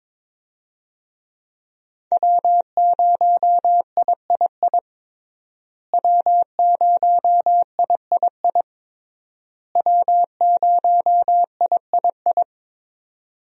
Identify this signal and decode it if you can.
Morse code, heard as a keyed tone, about 22 wpm, 700 Hz. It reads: W0III W0III W0III
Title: USA Call Signs - 5 Characters - Repeated 3x - 22wpm